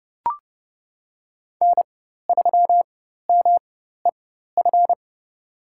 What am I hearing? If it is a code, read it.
Morse code, heard as a keyed tone, about 30 wpm, spaced as for 14 wpm, 690 Hz. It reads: N3MEF
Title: USA Call Signs - 4x Character Spacing - 30wpm